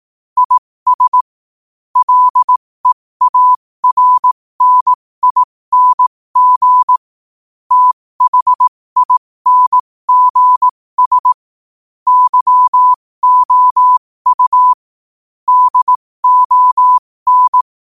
Morse code, heard as a keyed tone, about 18 wpm, spaced as for 15 wpm, 1000 Hz. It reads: IS LEARNING THINGS YOU DON
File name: qotdfiles / 15wpm